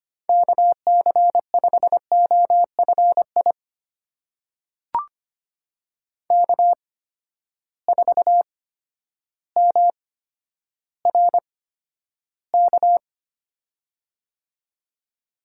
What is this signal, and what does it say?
Morse code, heard as a keyed tone, about 25 wpm, spaced as for 11 wpm, 700 Hz. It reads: KC5OFI K4MRK
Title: USA Call Signs - 8x Character Spacing - 25wpm